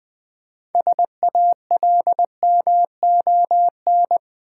Morse code, heard as a keyed tone, about 20 wpm, 690 Hz. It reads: SALMON
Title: Easy Words with Hints - 20wpm